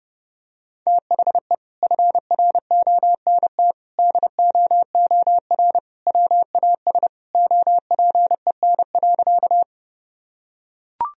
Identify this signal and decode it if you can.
Morse code, heard as a keyed tone, about 30 wpm, 700 Hz. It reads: THE FRONT DOOR WAS OPEN.
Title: Sentences from Top 1000 Words - Review Part 2 - 30wpm